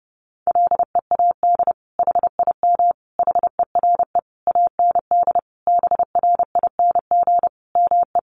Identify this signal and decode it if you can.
Morse code, heard as a keyed tone, about 30 wpm, 700 Hz. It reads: LEAD HIM HERE AND BRING ME
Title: Sentences from Top 1200 Words 30wpm